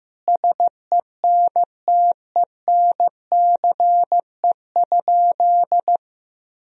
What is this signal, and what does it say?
Morse code, heard as a keyed tone, about 15 wpm, 700 Hz. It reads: SENTENCE?
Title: Sentences from Top 1000 Words - Review Part 1 - 15wpm